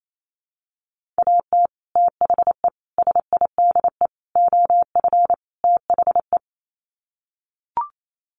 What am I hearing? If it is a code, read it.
Morse code, heard as a keyed tone, about 28 wpm, 700 Hz. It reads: AT THE SIDE OF THE
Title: Sets of 5 Words Encore2 28wpm